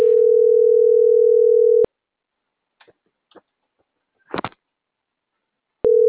Ringback tone (North America), 440 + 480 Hz.